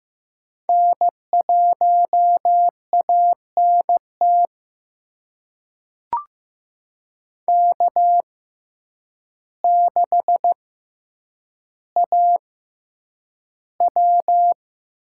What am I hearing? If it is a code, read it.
Morse code, heard as a keyed tone, about 15 wpm, spaced as for 8 wpm, 700 Hz. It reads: N1ANT K6AW
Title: USA Call Signs - 6x Character Spacing - 15wpm